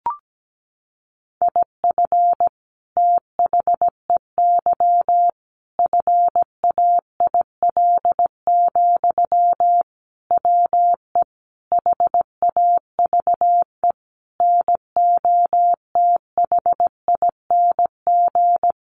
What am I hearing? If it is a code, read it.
Morse code, heard as a keyed tone, about 17 wpm, 700 Hz. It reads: IF THEY FAIL, WE HAVE NOTHING
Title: Sentences from Top 1100 Words 17wpm